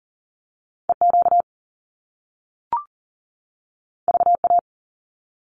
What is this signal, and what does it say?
Morse code, heard as a keyed tone, about 41 wpm, 700 Hz. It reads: EQ VA
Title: Top Two Letter Combinations in Words 40wpm